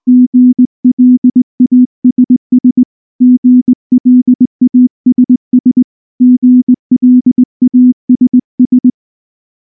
Morse code, heard as a keyed tone, about 18 wpm, 250 Hz. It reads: GLASS GLASS GLASS
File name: morse / morse-training-20WPM-1